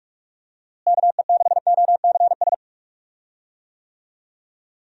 Morse code, heard as a keyed tone, about 42 wpm, 700 Hz. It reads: KE6OCS
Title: USA Call Signs - 6 Characters - Repeated 3x - 45wpm